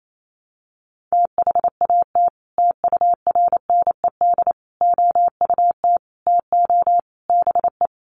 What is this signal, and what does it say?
Morse code, heard as a keyed tone, about 28 wpm, 700 Hz. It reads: THAT TURNED OUT TO BE
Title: Sets of 5 Words Encore2 28wpm